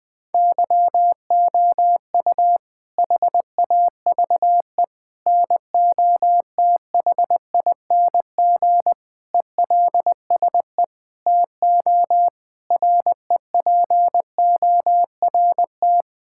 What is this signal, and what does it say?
Morse code, heard as a keyed tone, about 21 wpm, 700 Hz. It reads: YOU HAVE NOTHING ELSE TO REPORT
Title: Sentences from Top 1000 Words - Review Part 6 - 20wpm